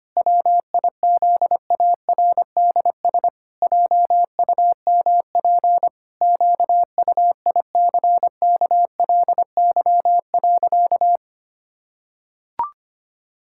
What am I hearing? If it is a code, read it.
Morse code, heard as a keyed tone, about 25 wpm, 700 Hz. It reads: WIZARDS JUMP QUICKLY.
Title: Pangram Challenge v2 - 25wpm